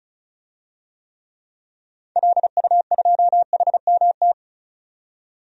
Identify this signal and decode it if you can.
Morse code, heard as a keyed tone, about 34 wpm, 700 Hz. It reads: LU2HMT
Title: DXCC-100 Call Signs - 6 Characters - 35wpm